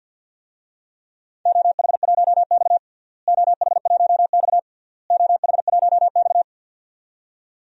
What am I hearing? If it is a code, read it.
Morse code, heard as a keyed tone, about 52 wpm, 700 Hz. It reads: OH1X OH1X OH1X
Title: DXCC-100 Call Signs - 4 Characters - Repeated 3x - 50wpm